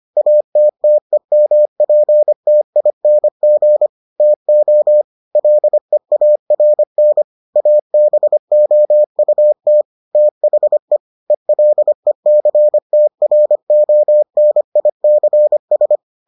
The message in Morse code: ATTEMPTING TO LEARN ABOUT THE ELECTRONICS

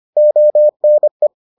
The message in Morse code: ONE

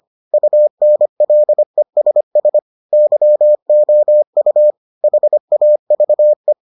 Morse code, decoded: UNLESS YOU HAVE

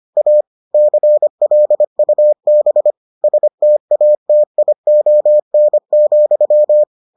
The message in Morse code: A CLUB STATION,

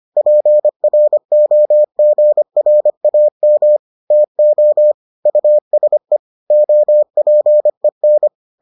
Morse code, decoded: PROGRAM TO USE OPEN

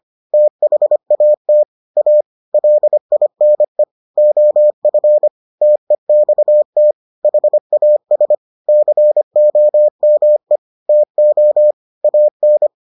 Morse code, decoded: THAT A LINE OF TEXT HAS COME TO AN

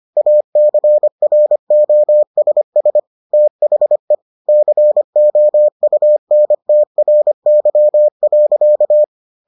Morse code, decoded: ACROSS THE COUNTRY.

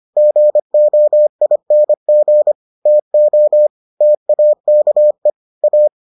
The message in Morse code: GOING TO TAKE A

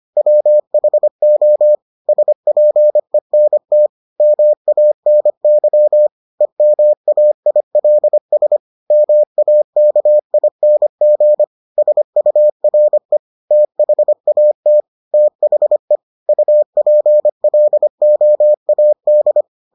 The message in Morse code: WHO SPENT MANY EMAILS MAKING SURE THAT THE UPLOAD

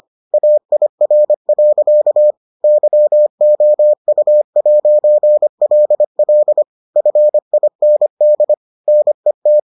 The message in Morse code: AIR. YOU'LL FIND NET

